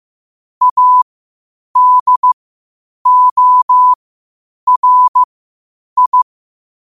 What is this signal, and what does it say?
Morse code, heard as a keyed tone, about 15 wpm, spaced as for 8 wpm, 1000 Hz. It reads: ADORI
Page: Quotes for Fri, 15 Aug 2025 in Morse Code at 8 words per minute.